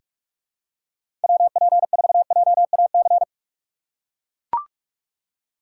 Morse code, heard as a keyed tone, about 44 wpm, 700 Hz. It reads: WP4JAC
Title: USA Call Signs - 6 Characters - 6x Character Spacing - 45wpm